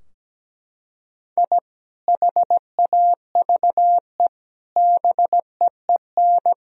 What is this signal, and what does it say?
Morse code, heard as a keyed tone, about 17 wpm, 700 Hz. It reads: I HAVE BEEN
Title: Sentences from Top 1000 Words - Review Part 3 - 17wpm